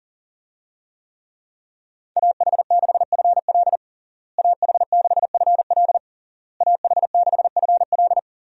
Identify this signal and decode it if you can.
Morse code, heard as a keyed tone, about 37 wpm, 700 Hz. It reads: AH6FL AH6FL AH6FL
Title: USA Call Signs - 5 Characters - Repeated 3x - 40wpm